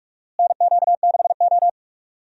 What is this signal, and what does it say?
Morse code, heard as a keyed tone, about 43 wpm, 700 Hz. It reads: NQ6O
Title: USA Call Signs - 4 Characters - Repeated 3x - 45wpm